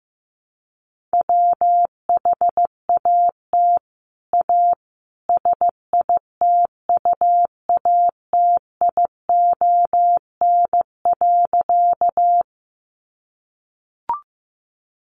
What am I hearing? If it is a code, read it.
Morse code, heard as a keyed tone, about 15 wpm, 700 Hz. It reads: WHAT A SITUATION.
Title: Sentences from Top 1200 Words 15wpm